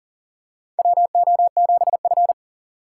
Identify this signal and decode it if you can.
Morse code, heard as a keyed tone, about 40 wpm, 700 Hz. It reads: WO7F
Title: USA Call Signs - 4 Characters - 4x Character Spacing - 40wpm